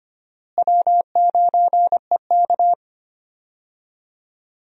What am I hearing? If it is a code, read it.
Morse code, heard as a keyed tone, about 25 wpm, 700 Hz. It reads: W9EK
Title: USA Call Signs - 4 Characters - Repeated 2x - 25wpm